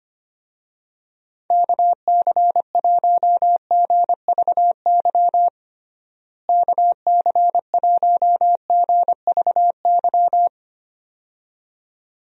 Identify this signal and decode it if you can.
Morse code, heard as a keyed tone, about 25 wpm, 700 Hz. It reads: KC1GVY KC1GVY
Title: USA Call Signs - 6 Characters - Repeated 2x - 25wpm